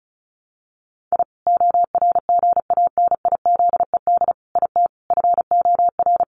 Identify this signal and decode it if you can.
Morse code, heard as a keyed tone, about 34 wpm, 700 Hz. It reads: I ORGANIZED IT FOR
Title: Sentences from Top 1300 Words 35wpm